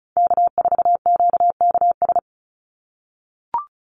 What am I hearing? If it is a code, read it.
Morse code, heard as a keyed tone, about 34 wpm, 700 Hz. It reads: K4QKS
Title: USA Call Signs 35wpm